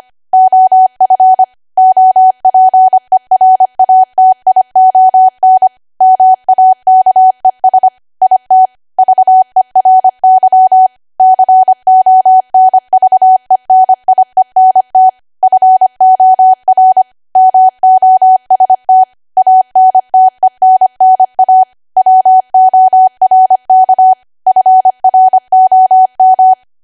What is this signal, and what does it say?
Morse code, heard as a keyed tone, about 25 wpm, 750 Hz. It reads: OF OPERATION MAKES IT VERY CONVENIENT FOR MOST ANTENNA WORK FROM
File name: Archive / 25 WPM / 25WPM